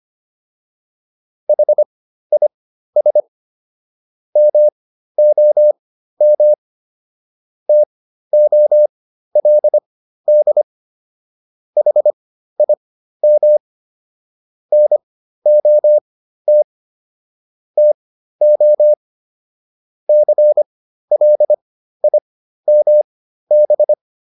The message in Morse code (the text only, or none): HIS MOM TOLD HIM NOT TO CLIMB